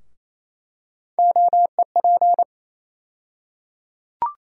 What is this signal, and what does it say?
Morse code, heard as a keyed tone, about 28 wpm, 700 Hz. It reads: OEP
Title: USA Call Sign Postfixes - 3 Letters 28wpm